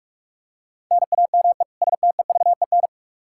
Morse code, heard as a keyed tone, about 44 wpm, 700 Hz. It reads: NAME STEVEN
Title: Practice QSO - Exchange 1 - Beginning - 45wpm